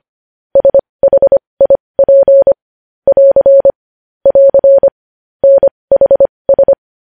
Morse code, heard as a keyed tone, about 25 wpm, spaced as for 20 wpm, 550 Hz. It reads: SHIP <AR> <AR> NHS
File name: podcast / News headlines-1x-25x20